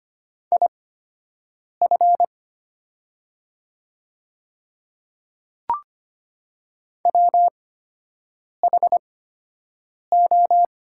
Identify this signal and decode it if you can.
Morse code, heard as a keyed tone, about 25 wpm, spaced as for 7 wpm, 690 Hz. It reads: IF WHO
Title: 8x Character Spacing - Top 300 Words - 25wpm